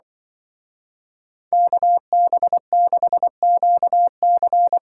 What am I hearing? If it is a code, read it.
Morse code, heard as a keyed tone, about 24 wpm, 700 Hz. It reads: KB6QC
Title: Speed Racing - USA 2x2 Call Signs 20wpm